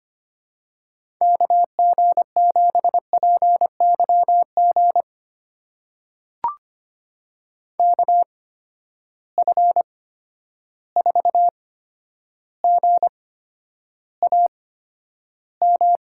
Morse code, heard as a keyed tone, about 25 wpm, spaced as for 11 wpm, 700 Hz. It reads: KG7PYG KF4GAM
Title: USA Call Signs - 6 Characters - 8x Character Spacing - 25wpm